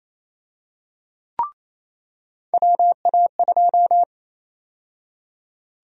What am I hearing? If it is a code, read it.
Morse code, heard as a keyed tone, about 28 wpm, 700 Hz. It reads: WA2
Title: USA Call Sign Prefixes - 28wpm